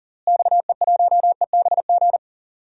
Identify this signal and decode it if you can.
Morse code, heard as a keyed tone, about 37 wpm, 700 Hz. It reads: XE1EBG
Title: DXCC-100 Call Signs - 6 Characters - Repeated 2x - 40wpm